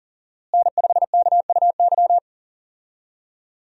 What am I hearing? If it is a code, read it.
Morse code, heard as a keyed tone, about 37 wpm, 700 Hz. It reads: N5KUY
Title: USA Call Signs - 5 Characters - Repeated 3x - 40wpm